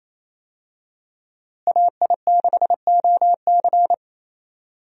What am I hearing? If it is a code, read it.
Morse code, heard as a keyed tone, about 28 wpm, 700 Hz. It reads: AI6OC
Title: USA Call Signs - 5 Characters - Repeated 2x - 28wpm